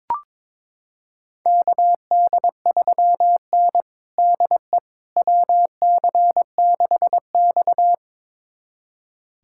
Morse code, heard as a keyed tone, about 22 wpm, 700 Hz. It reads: KD3N DE WC6X
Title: USA Callsign DE Callsign - 4 Characters - 22wpm